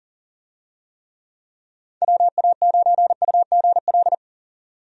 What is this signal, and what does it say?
Morse code, heard as a keyed tone, about 40 wpm, 700 Hz. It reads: WA9UGL
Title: USA Call Signs - 6 Characters - 40wpm